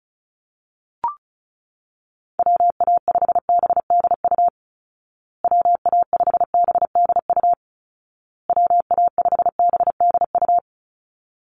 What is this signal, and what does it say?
Morse code, heard as a keyed tone, about 34 wpm, 700 Hz. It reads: WA5BDU WA5BDU WA5BDU
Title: Call Signs Repeated 2x - 35wpm